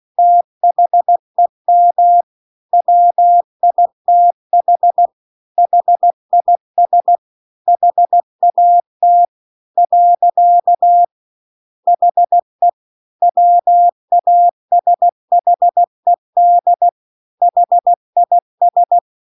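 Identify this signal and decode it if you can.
Morse code, heard as a keyed tone, about 16 wpm, 700 Hz. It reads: THEM WITH HIS HAT . HE WASHED HIS
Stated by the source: Morse practice text at 16 WPM.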